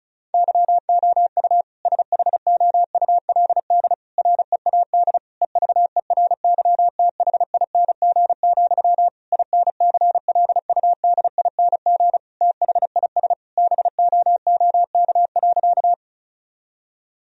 Morse code, heard as a keyed tone, about 36 wpm, 700 Hz. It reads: YOU SHOULD READ EVERYTHING, INCLUDING THIS BOOK.
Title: Sentences from Top 1000 Words - Review Part 4 - 35wpm